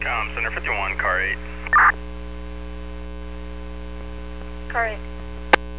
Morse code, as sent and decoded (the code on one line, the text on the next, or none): none